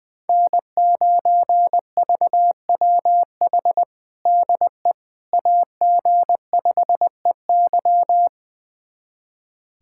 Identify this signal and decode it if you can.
Morse code, heard as a keyed tone, about 20 wpm, 700 Hz. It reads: N9VWH DE AG5EY
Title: USA Callsign DE Callsign - 5 Characters - 20wpm